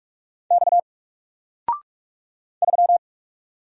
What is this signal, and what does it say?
Morse code, heard as a keyed tone, about 44 wpm, 700 Hz. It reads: X 3
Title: Single Letter-Number 45wpm